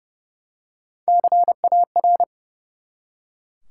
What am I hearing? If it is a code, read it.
Morse code, heard as a keyed tone, about 30 wpm, 700 Hz. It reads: CAR
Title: Three Letter Words - 30wpm